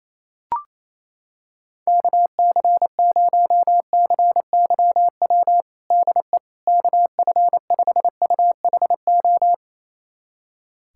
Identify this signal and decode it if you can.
Morse code, heard as a keyed tone, about 27 wpm, 700 Hz. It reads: KC0CYW DE KF5UHO
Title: USA Callsign DE Callsign - 6 Characters - 28wpm